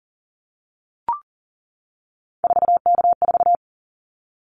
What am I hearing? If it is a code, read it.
Morse code, heard as a keyed tone, about 40 wpm, 700 Hz. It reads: 4K4
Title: Callsign Prefixes for Top 100 DXCC - 40wpm